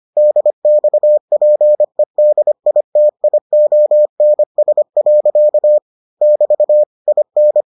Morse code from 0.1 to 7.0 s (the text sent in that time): DXPEDITIONS. <BT>